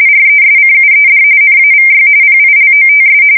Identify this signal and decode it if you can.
RTTY, about 45.45 baud. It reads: RESCUED NAVY PILOTS
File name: fsk170 / battleshipnj